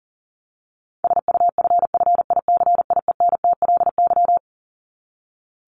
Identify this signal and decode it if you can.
Morse code, heard as a keyed tone, about 40 wpm, 700 Hz. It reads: SUFFICIENTLY
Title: LY Suffix - Top 100 Words with Base Words 40wpm